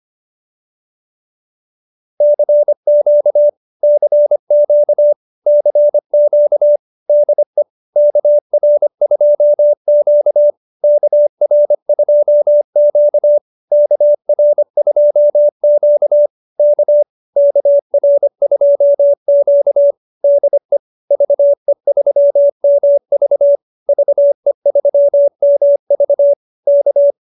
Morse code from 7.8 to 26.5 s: KR2Q KR2Q KR2Q K KR2Q DE VE3MV VE3MV

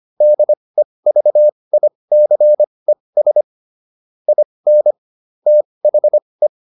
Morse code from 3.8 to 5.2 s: IN

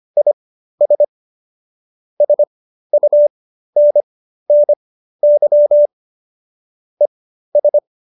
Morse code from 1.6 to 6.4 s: SUNNY

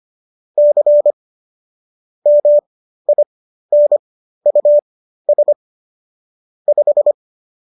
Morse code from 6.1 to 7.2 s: 5

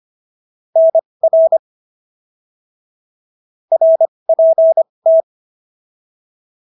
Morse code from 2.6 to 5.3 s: RPT

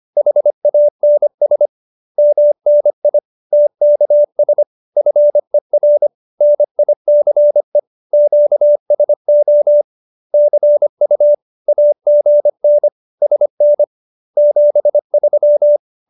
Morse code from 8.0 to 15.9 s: QSO CU AGN SN 73

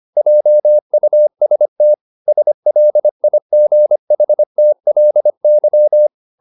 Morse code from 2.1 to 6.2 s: SLIGHTLY

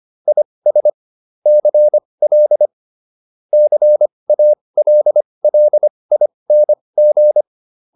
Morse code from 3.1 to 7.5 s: CALLING

